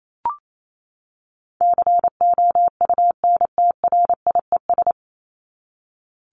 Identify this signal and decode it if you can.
Morse code, heard as a keyed tone, about 28 wpm, 700 Hz. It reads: COUNTRIES
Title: ES:S Suffix - Top 100 Words with Base Words 28wpm